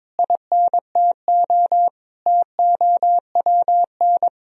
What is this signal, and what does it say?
Morse code, heard as a keyed tone, about 22 wpm, 700 Hz. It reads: INTO TOWN
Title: Sentences from Top 1000 Words - Review Part 2 - 22wpm